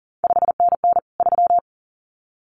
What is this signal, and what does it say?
Morse code, heard as a keyed tone, about 42 wpm, 700 Hz. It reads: 5NN 3
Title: CQ WW DX Contest Exchange Encore - 40wpm